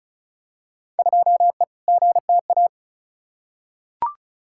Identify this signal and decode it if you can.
Morse code, heard as a keyed tone, about 36 wpm, 700 Hz. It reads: 2E GTA
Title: Field Day Exchanges Encore - 35wpm